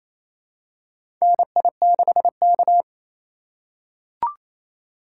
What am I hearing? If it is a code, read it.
Morse code, heard as a keyed tone, about 28 wpm, 700 Hz. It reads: NI6K
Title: USA Call Signs - 4 Characters - 4x Character Spacing - 28wpm